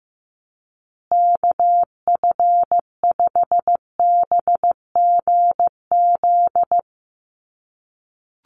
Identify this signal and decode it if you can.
Morse code, heard as a keyed tone, about 15 wpm, 700 Hz. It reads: KF5BGZ
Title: USA Call Signs Repeated 3x 15wpm